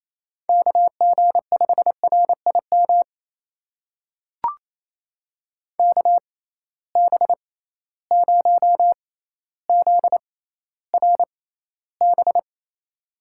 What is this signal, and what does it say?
Morse code, heard as a keyed tone, about 28 wpm, spaced as for 14 wpm, 700 Hz. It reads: KG5RIM KB0ZRB
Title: USA Call Signs - 6 Characters - 6x Character Spacing - 28wpm